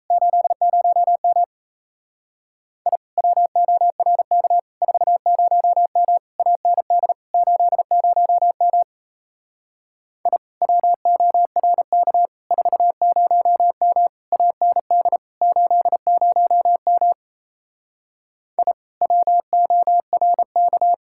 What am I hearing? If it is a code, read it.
Morse code, heard as a keyed tone, about 36 wpm, 700 Hz. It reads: 80M I WORK 40M AND 80M I WORK 40M AND 80M I WORK